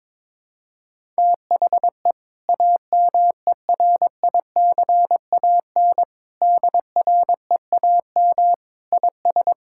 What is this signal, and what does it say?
Morse code, heard as a keyed tone, about 22 wpm, 700 Hz. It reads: THE AMERICAN DREAM IS
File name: practice / 22wpm / Sentences with 2-Word Phrases - 22wpm